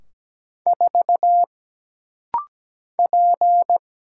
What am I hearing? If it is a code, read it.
Morse code, heard as a keyed tone, about 17 wpm, 700 Hz. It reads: 4 P
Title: Single Letter-Number Alt - Rapid-Fire 17wpm